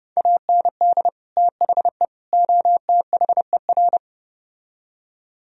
Morse code, heard as a keyed tone, about 31 wpm, 700 Hz. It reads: AND THE OTHER